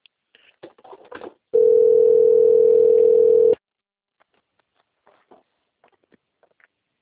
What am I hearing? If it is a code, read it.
Ringback tone (North America), 440 + 480 Hz.
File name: rg-1600-unknown- - - .25113